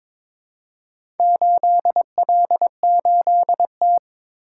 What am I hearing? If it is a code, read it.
Morse code, heard as a keyed tone, about 22 wpm, 700 Hz. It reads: 8L8T
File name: practice / 22wpm / Four Letter-Number Combinations - 22wpm